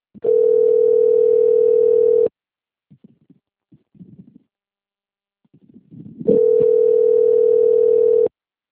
Ringback tone (North America), 440 + 480 Hz.